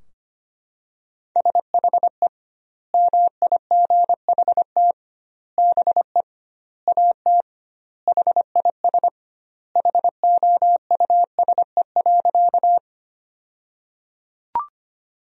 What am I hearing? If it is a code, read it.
Morse code, heard as a keyed tone, about 25 wpm, 700 Hz. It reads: SHE MIGHT BE AT HIS HOUSE.
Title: 2x Word-Spacing - Sentences from Top 300 Words - 25wpm